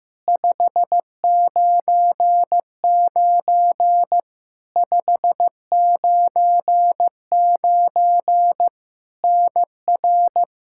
Morse code, heard as a keyed tone, about 15 wpm, 700 Hz. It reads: 599 599 NR
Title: Practice QSO - Exchange 1 - Beginning - 15wpm